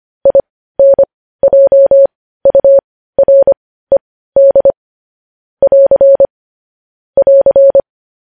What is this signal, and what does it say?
Morse code, heard as a keyed tone, about 25 wpm, spaced as for 15 wpm, 550 Hz. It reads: INJURED <AR> <AR>
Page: news headlines in morse code at 15 wpm